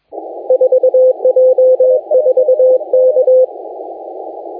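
Morse code, heard as a keyed tone, about 21 wpm, 510 Hz. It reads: 4J4K
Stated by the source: Morse contest audio recorded by LA4C during the SAC CW 2018 contest.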